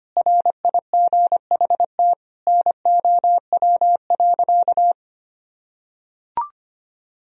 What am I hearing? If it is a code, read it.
Morse code, heard as a keyed tone, about 25 wpm, 700 Hz. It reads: RIGHT NOW.
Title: Sentences from Top 1000 Words - Review Part 4 - 25wpm